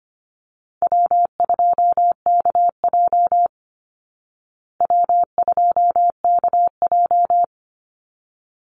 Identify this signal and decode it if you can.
Morse code, heard as a keyed tone, about 25 wpm, 700 Hz. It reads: W2KJ W2KJ
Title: Call Signs Repeated 3x - 25wpm